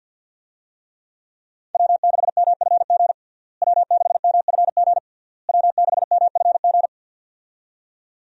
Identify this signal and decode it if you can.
Morse code, heard as a keyed tone, about 46 wpm, 700 Hz. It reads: W6MFG W6MFG W6MFG
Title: USA Call Signs - 5 Characters - Repeated 3x - 50wpm